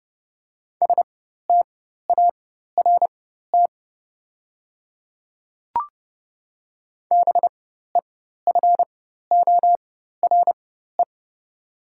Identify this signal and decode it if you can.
Morse code, heard as a keyed tone, about 30 wpm, spaced as for 14 wpm, 700 Hz. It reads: START BEFORE
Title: 4x Character Spacing - Top 300 Words - 30wpm